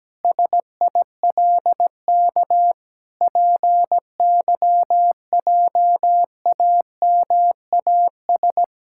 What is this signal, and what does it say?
Morse code, heard as a keyed tone, about 17 wpm, 700 Hz. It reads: SILK PYJAMAS
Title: Pangram Challenge v2 - 17wpm